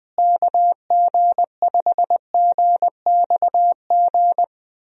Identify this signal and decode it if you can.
Morse code, heard as a keyed tone, about 20 wpm, 700 Hz. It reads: KG5GXG
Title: USA Call Signs - 6 Characters - Repeated 2x - 20wpm